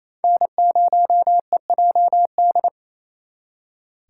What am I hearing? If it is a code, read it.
Morse code, heard as a keyed tone, about 35 wpm, 700 Hz. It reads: N0EJD
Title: Speed Racing - USA 1x3 Call Signs 28wpm